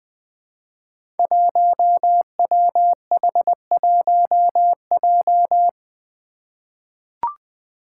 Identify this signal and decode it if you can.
Morse code, heard as a keyed tone, about 20 wpm, 700 Hz. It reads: 1WH1J
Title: Five Challenging Letter-Number Combinations 20wpm